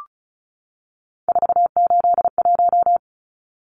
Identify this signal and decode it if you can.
Morse code, heard as a keyed tone, about 34 wpm, 700 Hz. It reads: 481
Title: Numbers - 3-Digit 35wpm